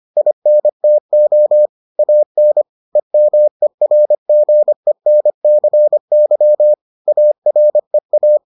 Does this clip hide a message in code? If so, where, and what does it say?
Yes: Morse code from 0.2 to 8.4 s, heard as a keyed tone, about 25 wpm, 600 Hz. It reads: INTO AN EMERGENCY AREA